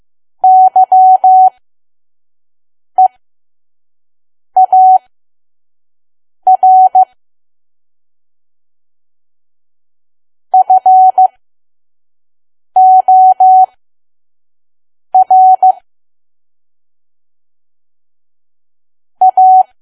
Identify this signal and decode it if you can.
Morse code, heard as a keyed tone, about 15 wpm, spaced as for 5 wpm, 750 Hz. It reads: YEAR FOR A
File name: Archive / 5 WPM files / PM